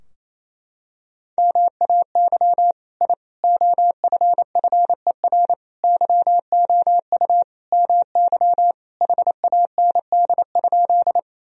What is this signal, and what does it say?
Morse code, heard as a keyed tone, about 28 wpm, 700 Hz. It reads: MAY I OFFER YOU MY HAND?
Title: Sentences from Top 1000 Words - Review Part 4 - 28wpm